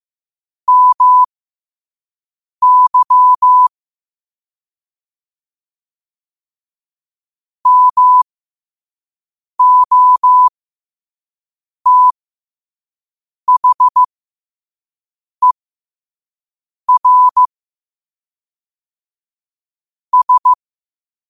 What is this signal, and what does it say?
Morse code, heard as a keyed tone, about 15 wpm, spaced as for 5 wpm, 1000 Hz. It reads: MY MOTHER S
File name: qotdfiles / 5wpm